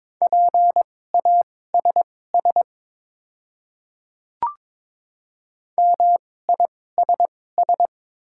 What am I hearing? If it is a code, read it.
Morse code, heard as a keyed tone, about 22 wpm, spaced as for 16 wpm, 700 Hz. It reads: PASS MISS
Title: 2x Character Spacing - Top 500 Words - 22wpm